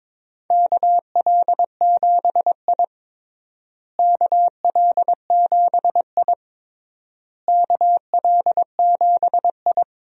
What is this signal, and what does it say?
Morse code, heard as a keyed tone, about 22 wpm, 700 Hz. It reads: KL7I KL7I KL7I
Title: DXCC-100 Call Signs - 4 Characters - Repeated 3x - 22wpm